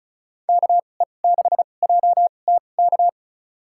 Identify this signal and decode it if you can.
Morse code, heard as a keyed tone, about 36 wpm, spaced as for 25 wpm, 700 Hz. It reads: KE6JTK
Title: USA Call Signs - 2x Character Spacing - 35wpm